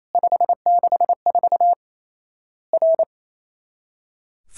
Morse code, heard as a keyed tone, about 28 wpm, 700 Hz. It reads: 564 R
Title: Worked All Europe DX Contest - QTC Exchange - 28wpm